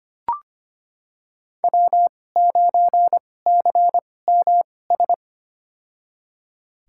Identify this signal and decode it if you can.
Morse code, heard as a keyed tone, about 25 wpm, spaced as for 18 wpm, 700 Hz. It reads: W9CMS
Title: USA Call Signs - 5 Characters - 2x Character Spacing - 25wpm